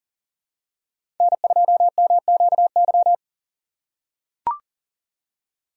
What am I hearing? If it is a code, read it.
Morse code, heard as a keyed tone, about 39 wpm, 700 Hz. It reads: N2MQY
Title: USA Call Signs - 5 Characters - 2x Character Spacing - 40wpm